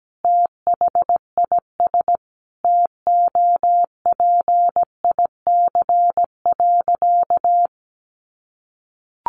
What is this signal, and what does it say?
Morse code, heard as a keyed tone, about 17 wpm, 700 Hz. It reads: THIS TOPIC.
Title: Sentences from Top 1200 Words 17wpm